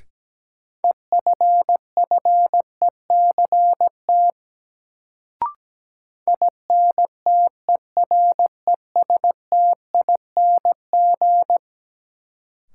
Morse code, heard as a keyed tone, about 17 wpm, 700 Hz. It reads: EFFECT INTERESTING
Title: Sentences from Top 1000 Words - New Word Drill - 17wpm